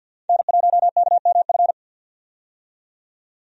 Morse code, heard as a keyed tone, about 50 wpm, 700 Hz. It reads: N1KMF
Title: USA Call Signs - 5 Characters - 50wpm